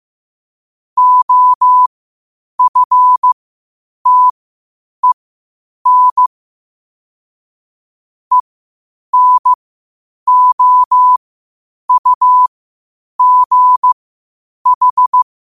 Morse code, heard as a keyed tone, about 15 wpm, spaced as for 8 wpm, 1000 Hz. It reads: OFTEN ENOUGH
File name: qotdfiles / 8wpm